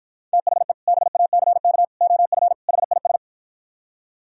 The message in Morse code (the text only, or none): THE BACK OF HIS